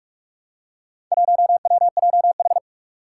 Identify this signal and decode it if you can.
Morse code, heard as a keyed tone, about 46 wpm, 700 Hz. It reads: 1WJH
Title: Four Challenging Letter-Number Combinations 45wpm